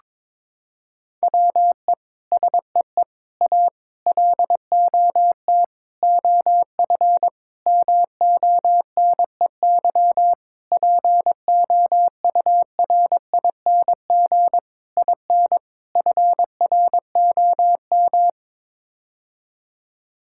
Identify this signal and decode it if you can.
Morse code, heard as a keyed tone, about 21 wpm, 700 Hz. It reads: WE SEE A LOT OF MONEY POURING IN FROM
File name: Sets of 9 Words - 22wpm